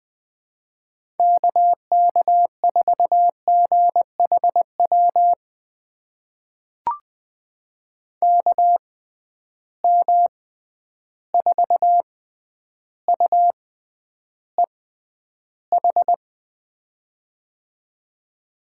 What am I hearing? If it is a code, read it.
Morse code, heard as a keyed tone, about 20 wpm, spaced as for 10 wpm, 700 Hz. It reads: KK4GHW KM4UEH
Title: USA Call Signs - 6 Characters - 6x Character Spacing - 20wpm